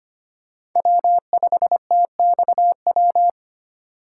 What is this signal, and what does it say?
Morse code, heard as a keyed tone, about 25 wpm, 700 Hz. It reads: W5TXW
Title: USA Call Signs - 5 Characters - 4x Character Spacing - 25wpm